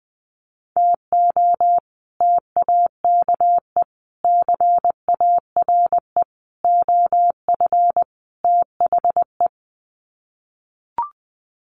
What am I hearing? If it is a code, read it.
Morse code, heard as a keyed tone, about 20 wpm, 700 Hz. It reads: TO TAKE CARE OF THE
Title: Sets of 5 Words Encore 20wpm